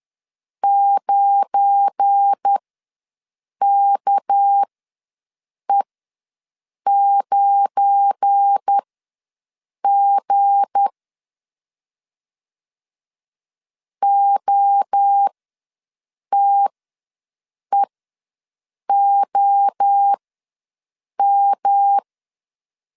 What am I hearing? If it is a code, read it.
Morse code, heard as a keyed tone, about 10 wpm, spaced as for 6 wpm, 800 Hz. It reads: 9KE9G OTEOM
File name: LEZIONI / 7 GF9Z / 7 GF9 ws 10wpm 4034